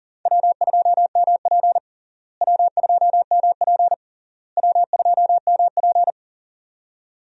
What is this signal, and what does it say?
Morse code, heard as a keyed tone, about 40 wpm, 700 Hz. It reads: W2MP W2MP W2MP
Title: USA Call Signs - 4 Characters - Repeated 3x - 40wpm